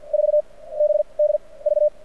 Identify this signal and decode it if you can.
Morse code, heard as a keyed tone, about 48 wpm, spaced as for 23 wpm, 600 Hz. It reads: 4BDV
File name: archived / CRSnet